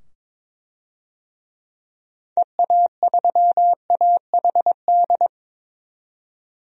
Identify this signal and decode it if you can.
Morse code, heard as a keyed tone, about 22 wpm, 700 Hz. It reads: EA3AHD
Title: DXCC-100 Call Signs - 6 Characters - 22wpm